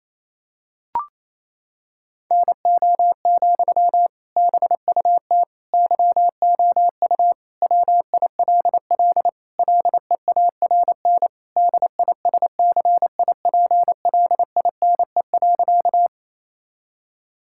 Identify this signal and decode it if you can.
Morse code, heard as a keyed tone, about 28 wpm, 700 Hz. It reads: NO, BUT YOU WILL LEARN DISCIPLINE.
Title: Sentences from Top 1000 Words - Review Part 6 - 28wpm